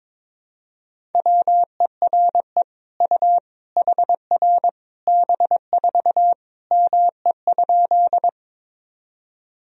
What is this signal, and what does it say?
Morse code, heard as a keyed tone, about 22 wpm, 700 Hz. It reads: WERE U HR B4 ME?
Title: Sentences from Top 200 Words and Abbreviations 22wpm